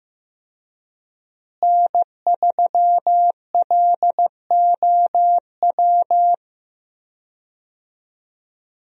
Morse code, heard as a keyed tone, about 15 wpm, 700 Hz. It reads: N3LOW
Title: USA Call Signs - 5 Characters - Repeated 3x - 15wpm